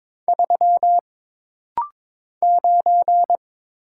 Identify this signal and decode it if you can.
Morse code, heard as a keyed tone, about 22 wpm, 700 Hz. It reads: 3 9
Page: This rapid-fire timing is geared to push your ICR (Instant Character Recognition) skills.